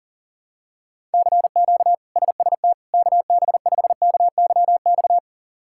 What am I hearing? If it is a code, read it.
Morse code, heard as a keyed tone, about 40 wpm, 700 Hz. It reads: CQ SST KB5KYX
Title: K1USN SST QSO Practice - 40wpm